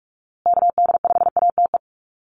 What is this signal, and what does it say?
Morse code, heard as a keyed tone, about 44 wpm, 700 Hz. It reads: KD5ATE
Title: USA Call Signs Repeated 2x 45wpm